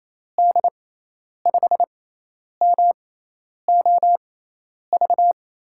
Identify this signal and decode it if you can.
Morse code, heard as a keyed tone, about 28 wpm, spaced as for 10 wpm, 700 Hz. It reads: D5MOV
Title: USA Call Signs - 6 Characters - 6x Character Spacing - 28wpm